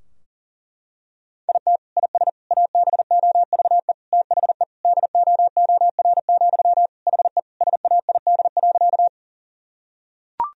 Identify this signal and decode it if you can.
Morse code, heard as a keyed tone, about 40 wpm, 700 Hz. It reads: IT IS ABOVE THE DOOR, HE SAID.
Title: Sentences from Top 1000 Words - Review Part 2 - 40wpm